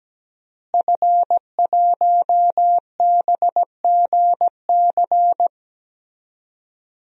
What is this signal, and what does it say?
Morse code, heard as a keyed tone, about 17 wpm, 700 Hz. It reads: F1BGC
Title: DXCC-100 Call Signs - 5 Characters - Repeated 3x - 17wpm